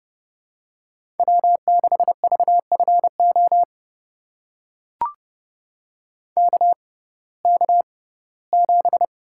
Morse code, heard as a keyed tone, about 30 wpm, spaced as for 18 wpm, 700 Hz. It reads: W6VFO KK7
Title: USA Call Signs - 5 Characters - 6x Character Spacing - 30wpm